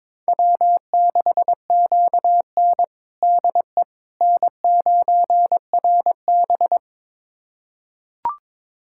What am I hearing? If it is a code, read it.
Morse code, heard as a keyed tone, about 22 wpm, 700 Hz. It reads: W6QN DE N9RB
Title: USA Callsign DE Callsign - 4 Characters - 22wpm